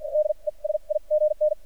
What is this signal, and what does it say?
Morse code, heard as a keyed tone, about 53 wpm, spaced as for 39 wpm, 610 Hz. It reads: DESIGN
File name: archived / CRSnet